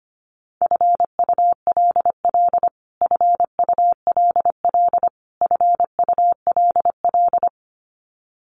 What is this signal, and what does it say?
Morse code, heard as a keyed tone, about 25 wpm, 700 Hz. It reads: FULL FULL FULL
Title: Top 500 Words Repeated 25wpm